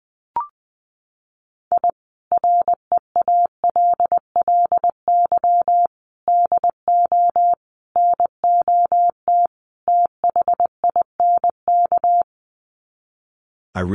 Morse code, heard as a keyed tone, about 20 wpm, 700 Hz. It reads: I REALLY DO NOT THINK
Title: Sets of 5 Words Encore2 20wpm